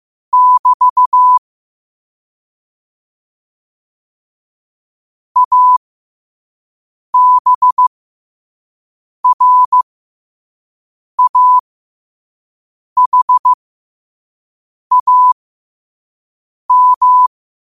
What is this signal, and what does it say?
Morse code, heard as a keyed tone, about 15 wpm, spaced as for 5 wpm, 1000 Hz. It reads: <BT> ABRAHAM
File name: qotdfiles / 5wpm